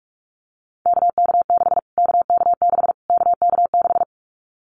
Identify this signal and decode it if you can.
Morse code, heard as a keyed tone, about 44 wpm, 700 Hz. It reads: KK6 KK6 KK6
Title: Callsign Prefixes for Top 25 DXCC - Repeated 3x - 45wpm